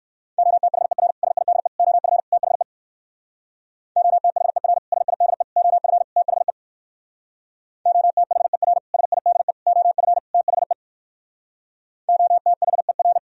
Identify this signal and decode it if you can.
Morse code, heard as a keyed tone, about 59 wpm, 700 Hz. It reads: OTHER SIDE OF THE OTHER SIDE OF THE OTHER SIDE OF THE OTHER